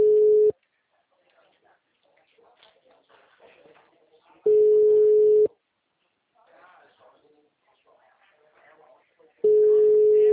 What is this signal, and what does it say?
Ringback tone (Europe, 425 Hz), with speech.
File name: out- -2039- - - .30193